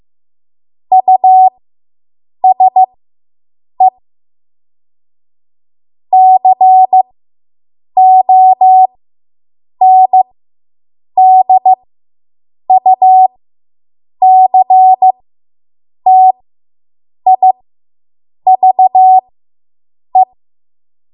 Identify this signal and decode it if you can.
Morse code, heard as a keyed tone, about 15 wpm, spaced as for 7 wpm, 750 Hz. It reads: USE CONDUCTIVE